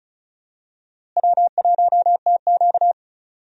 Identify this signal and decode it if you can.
Morse code, heard as a keyed tone, about 36 wpm, 700 Hz. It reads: W1TQ
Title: USA Call Signs - 4 Characters - Repeated 2x - 35wpm